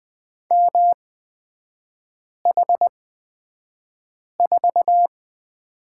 Morse code, heard as a keyed tone, about 20 wpm, spaced as for 5 wpm, 700 Hz. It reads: MH4
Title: 255 Lesson 41e - Period - Three Characters 20wpm